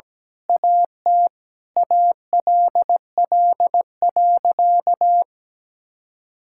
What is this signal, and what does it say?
Morse code, heard as a keyed tone, about 17 wpm, 700 Hz. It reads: AT ALL.
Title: Sentences from Top 1600 Words 17wpm